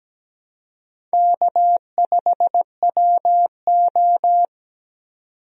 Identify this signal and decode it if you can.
Morse code, heard as a keyed tone, about 17 wpm, 700 Hz. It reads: K5WO
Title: USA Call Signs - 4 Characters - 17wpm